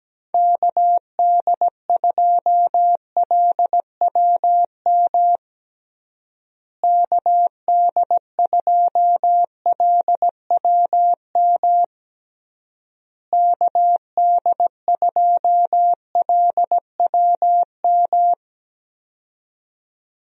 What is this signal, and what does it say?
Morse code, heard as a keyed tone, about 17 wpm, 700 Hz. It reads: KD2LWM KD2LWM KD2LWM
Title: USA Call Signs - 6 Characters - Repeated 3x - 17wpm